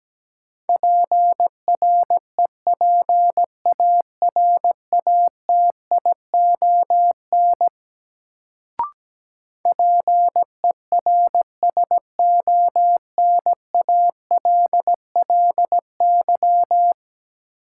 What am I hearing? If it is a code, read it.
Morse code, heard as a keyed tone, about 17 wpm, 700 Hz. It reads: PREPARATION PERSONALLY
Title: Sentences from Top 1400 Words - New Word Drill Alt - 17wpm